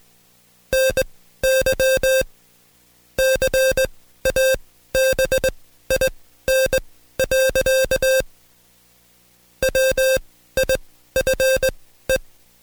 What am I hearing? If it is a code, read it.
Morse code, heard as a keyed tone, about 20 wpm, spaced as for 13 wpm, 530 Hz. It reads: NY CABIN. WIFE
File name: Code Quick / AudioDownld2 / Seg133-Stepper6 13WPM